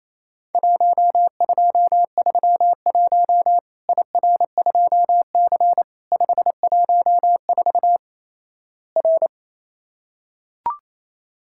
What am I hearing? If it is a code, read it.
Morse code, heard as a keyed tone, about 27 wpm, 700 Hz. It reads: 1231 IR2C 514 R
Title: Worked All Europe DX Contest - QTC Exchange - 28wpm